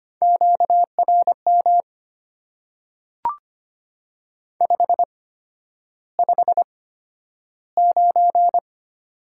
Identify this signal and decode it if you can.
Morse code, heard as a keyed tone, about 25 wpm, spaced as for 11 wpm, 700 Hz. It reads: QRM 559
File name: Top 90 QSO Elements - 8x Character Spacing - 25wpm